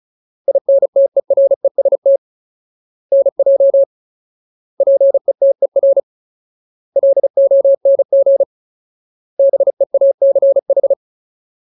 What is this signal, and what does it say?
Morse code, heard as a keyed tone, about 34 wpm, 550 Hz. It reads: INTEREST NJ PETER LONG BEACH
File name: Ultimate Word List / 35WPM Ultimate